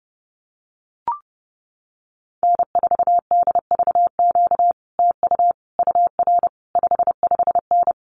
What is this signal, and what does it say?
Morse code, heard as a keyed tone, about 30 wpm, 700 Hz. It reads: N4DVQ TU UR 55N
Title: Common QSO Phrases Encore 30wpm